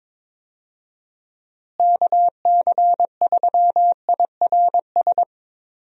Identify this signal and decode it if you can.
Morse code, heard as a keyed tone, about 22 wpm, 700 Hz. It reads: KC3IRS
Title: USA Call Signs - 6 Characters - Repeated 3x - 22wpm